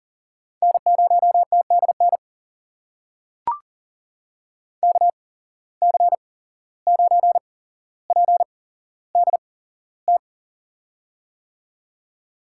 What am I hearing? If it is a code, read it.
Morse code, heard as a keyed tone, about 40 wpm, spaced as for 16 wpm, 700 Hz. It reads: N0TDN KC9PDT
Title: USA Call Signs - 8x Character Spacing - 40wpm